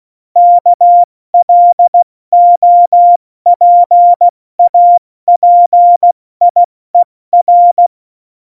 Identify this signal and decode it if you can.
Morse code, heard as a keyed tone, about 16 wpm, spaced as for 14 wpm, 700 Hz. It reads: KLOPAPIER
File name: Morsekurs / Witze / Witz 09. April 2019, 70 F80